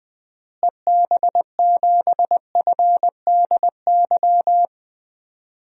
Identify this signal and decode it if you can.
Morse code, heard as a keyed tone, about 21 wpm, 700 Hz. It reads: EB7FDY
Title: DXCC-100 Call Signs - 6 Characters - Repeated 3x - 20wpm